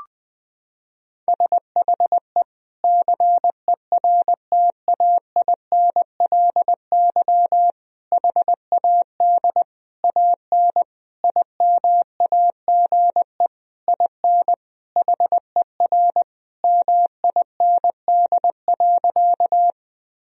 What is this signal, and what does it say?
Morse code, heard as a keyed tone, about 20 wpm, 700 Hz. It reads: SHE CERTAINLY HAD AN IMAGE IN HER MIND.
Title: Sentences from Top 1000 Words - Review Part 4 - 20wpm